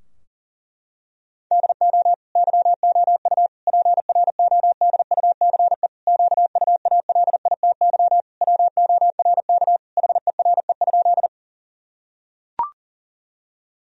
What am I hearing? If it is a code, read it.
Morse code, heard as a keyed tone, about 40 wpm, 700 Hz. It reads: DO YOU PRODUCE QUALITY WORK HERE?
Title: Sentences from Top 1000 Words - Review Part 4 - 40wpm